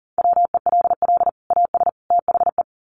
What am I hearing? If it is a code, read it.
Morse code, heard as a keyed tone, about 40 wpm, 700 Hz. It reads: WELL AS THE
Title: Sets of 3 Words 40wpm